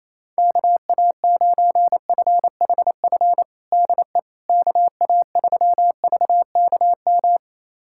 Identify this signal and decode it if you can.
Morse code, heard as a keyed tone, about 28 wpm, 700 Hz. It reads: KA9FHF DE KA3VKM
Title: USA Callsign DE Callsign - 6 Characters - 28wpm